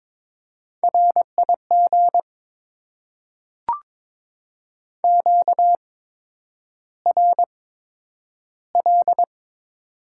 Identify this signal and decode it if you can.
Morse code, heard as a keyed tone, about 22 wpm, spaced as for 9 wpm, 700 Hz. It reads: RIG QRL
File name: Top 20 QSO Elements - 8x Character Spacing - 22wpm